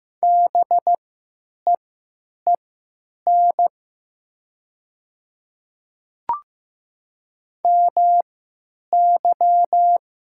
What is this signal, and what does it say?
Morse code, heard as a keyed tone, about 15 wpm, spaced as for 8 wpm, 700 Hz. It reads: BEEN MY
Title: 3x Character Spacing - Top 100 Words - 15wpm